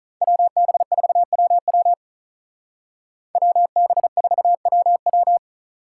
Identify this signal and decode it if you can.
Morse code, heard as a keyed tone, about 46 wpm, 700 Hz. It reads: WB4WW WB4WW